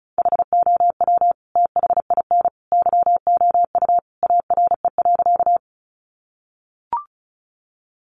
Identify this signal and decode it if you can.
Morse code, heard as a keyed tone, about 34 wpm, 700 Hz. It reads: HOW THIN YOU ARE.
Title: Sentences from Top 1200 Words 35wpm